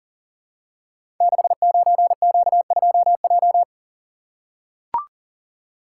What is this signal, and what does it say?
Morse code, heard as a keyed tone, about 40 wpm, 700 Hz. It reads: 69Q2J
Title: Code Groups - 5 Letter:Numbers - 40wpm